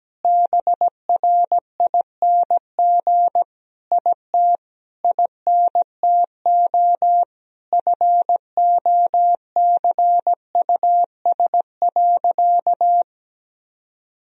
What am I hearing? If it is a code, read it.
Morse code, heard as a keyed tone, about 17 wpm, 700 Hz. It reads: BRING IT INTO FOCUS.
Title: Sentences from Top 1000 Words - Review Part 4 - 17wpm